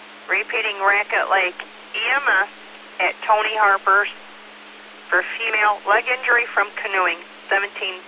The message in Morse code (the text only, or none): none